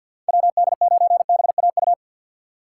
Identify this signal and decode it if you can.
Morse code, heard as a keyed tone, about 53 wpm, 700 Hz. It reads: WD9BAU
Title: USA Call Signs - 6 Characters - 8x Character Spacing - 50wpm